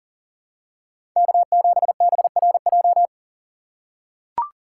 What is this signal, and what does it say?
Morse code, heard as a keyed tone, about 40 wpm, 700 Hz. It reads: K7BRJ
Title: USA Call Signs - 2x Character Spacing - 40wpm